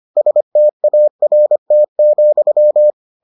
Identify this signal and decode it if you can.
Morse code, heard as a keyed tone, about 25 wpm, 600 Hz. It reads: START,